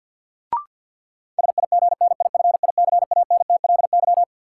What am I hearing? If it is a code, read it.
Morse code, heard as a keyed tone, about 52 wpm, 700 Hz. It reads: SIGNIFICANTLY
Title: Sentences from Top 1500 Words - New Word Drill Alt - 50wpm